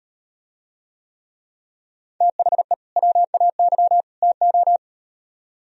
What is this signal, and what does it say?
Morse code, heard as a keyed tone, about 36 wpm, 700 Hz. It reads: THE WAY TO